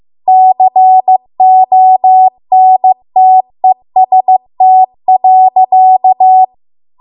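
Morse code, heard as a keyed tone, about 15 wpm, 750 Hz. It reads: CONTEST.
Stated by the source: Morse code practice at 15 wpm from the national amateur radio association's code archive.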